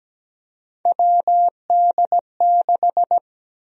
Morse code, heard as a keyed tone, about 17 wpm, 700 Hz. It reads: WD6
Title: USA Call Sign Prefixes - 17wpm